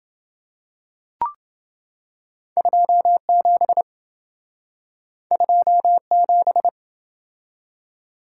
Morse code, heard as a keyed tone, about 27 wpm, 700 Hz. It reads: 27 27